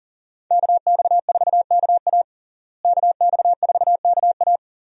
Morse code, heard as a keyed tone, about 40 wpm, 700 Hz. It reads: KX4KA KX4KA
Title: USA Call Signs - 5 Characters - Repeated 2x - 40wpm